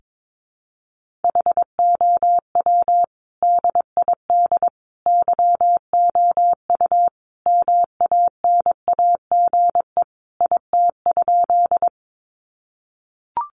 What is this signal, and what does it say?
Morse code, heard as a keyed tone, about 22 wpm, 700 Hz. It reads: HOW DID YOU MANAGE IT?
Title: Sentences from Top 1400 Words 22wpm